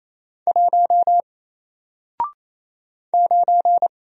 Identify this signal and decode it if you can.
Morse code, heard as a keyed tone, about 27 wpm, 700 Hz. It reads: 1 9
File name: Single-Letter-Number-28wpm